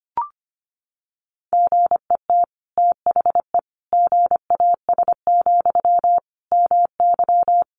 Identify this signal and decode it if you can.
Morse code, heard as a keyed tone, about 25 wpm, 700 Hz. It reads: GET THE GAS, MY
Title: Sentences from Top 1100 Words 25wpm